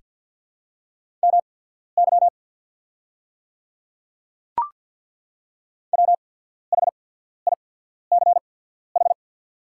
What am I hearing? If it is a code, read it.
Morse code, heard as a keyed tone, about 50 wpm, spaced as for 14 wpm, 690 Hz. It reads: MY WHICH
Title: 8x Character Spacing - Top 100 Words - 50wpm